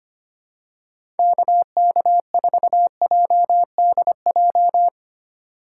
Morse code, heard as a keyed tone, about 25 wpm, 700 Hz. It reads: KK4JDJ
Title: USA Call Signs - 4x Character Spacing - 25wpm